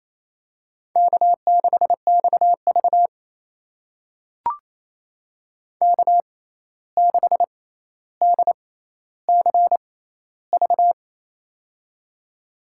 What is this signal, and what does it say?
Morse code, heard as a keyed tone, about 28 wpm, spaced as for 13 wpm, 700 Hz. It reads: K6XV K6DCV
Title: USA Call Signs - 6x Character Spacing - 28wpm